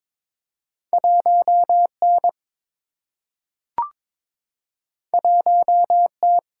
Morse code, heard as a keyed tone, about 22 wpm, 700 Hz. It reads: 1 N 1 T
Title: Single-Letter Single-Number 22wpm